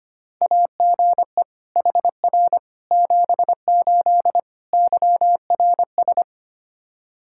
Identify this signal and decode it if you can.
Morse code, heard as a keyed tone, about 25 wpm, 700 Hz. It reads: AGE HR 78 YRS
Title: Practice QSO - Exchange 3 - More Info - 25wpm